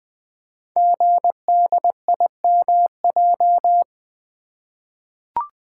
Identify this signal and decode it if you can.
Morse code, heard as a keyed tone, about 20 wpm, 700 Hz. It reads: GDIMJ
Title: Code Groups - 5 Letters - 20wpm